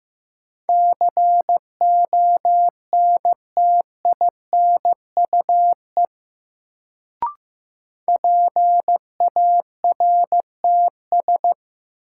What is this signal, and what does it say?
Morse code, heard as a keyed tone, about 15 wpm, 700 Hz. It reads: CONTINUE PARTS
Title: Sentences from Top 1000 Words - Review Part 6 - New Word Drill - 15wpm